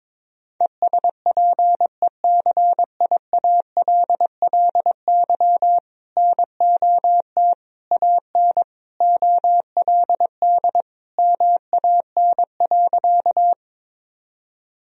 Morse code, heard as a keyed tone, about 21 wpm, 700 Hz. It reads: ESPECIALLY NOT AN OLD MAN.
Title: Sentences from Top 1000 Words - Review Part 4 - 22wpm